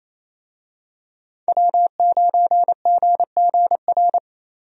Morse code, heard as a keyed tone, about 28 wpm, 700 Hz. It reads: W9GGR
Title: USA Call Signs - 5 Characters - 28wpm